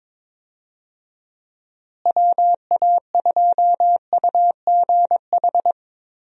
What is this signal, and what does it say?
Morse code, heard as a keyed tone, about 22 wpm, 700 Hz. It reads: WA2UGH
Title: USA Call Signs - 6 Characters - Repeated 2x - 22wpm